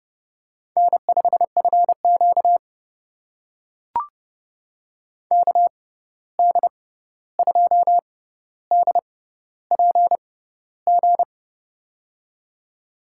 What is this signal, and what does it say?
Morse code, heard as a keyed tone, about 30 wpm, spaced as for 14 wpm, 700 Hz. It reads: N5FQ KD2DPG
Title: USA Call Signs - 6x Character Spacing - 30wpm